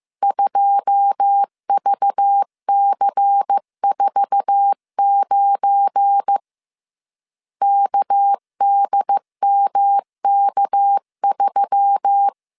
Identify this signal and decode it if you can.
Morse code, heard as a keyed tone, about 15 wpm, 800 Hz. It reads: 2VC49 KDMK3
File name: CW / DET / 1 9 1500C S C 14wpm 3245